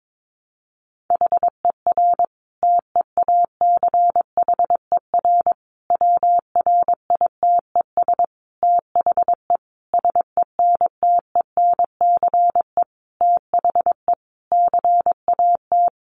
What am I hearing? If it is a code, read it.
Morse code, heard as a keyed tone, about 22 wpm, 700 Hz. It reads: HER TEACHER WRITES THE SENTENCE THE CAT
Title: Additional Easy Phrases 22wpm